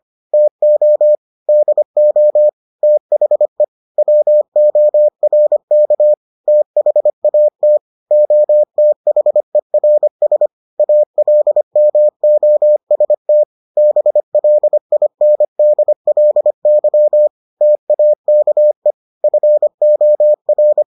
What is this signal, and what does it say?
Morse code, heard as a keyed tone, about 25 wpm, 600 Hz. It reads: TO DO THE WORK THAT OTHERS ALMOST BLINDLY TAKE FOR